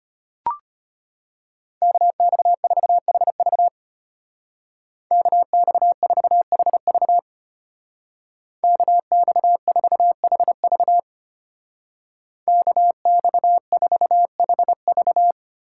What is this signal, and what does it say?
Morse code, heard as a keyed tone, about 30 wpm, 700 Hz. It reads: KX4HV KX4HV KX4HV KX4HV
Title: Speed Racing - USA 2x2 Call Signs 25wpm